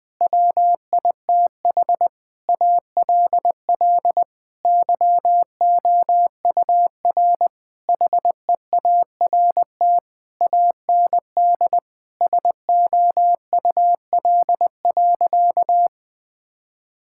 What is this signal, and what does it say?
Morse code, heard as a keyed tone, about 21 wpm, 700 Hz. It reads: WITH ALL YOUR HEART AND SOUL.
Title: Sentences with 9-Word Phrases - 20wpm